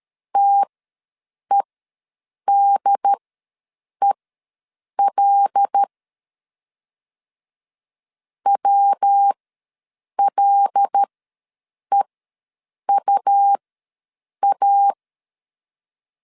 Morse code, heard as a keyed tone, about 13 wpm, spaced as for 7 wpm, 800 Hz. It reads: TEDEL WLEUA
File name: LEZIONI / 5 LBVC / 5 L ws 12wpm 2958